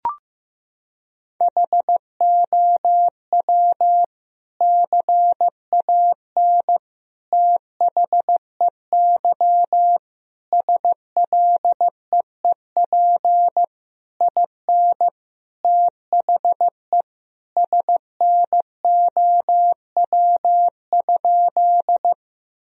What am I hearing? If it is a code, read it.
Morse code, heard as a keyed tone, about 15 wpm, 700 Hz. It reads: HOW CAN THEY SLEEP IN THE SNOW?
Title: Sentences from Top 1000 Words - Review Part 3 - 15wpm